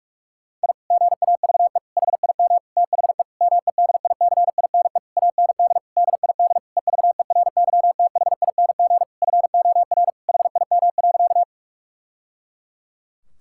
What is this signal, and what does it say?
Morse code, heard as a keyed tone, about 44 wpm, 700 Hz. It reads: I GAVE HIM THE MEDICINE AND DID EVERYTHING FOR HIM.
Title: Sentences from Top 1000 Words - Review Part 6 - 45wpm